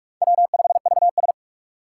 Morse code, heard as a keyed tone, about 44 wpm, 700 Hz. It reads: W5VS
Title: USA Call Signs - 6x Character Spacing - 45wpm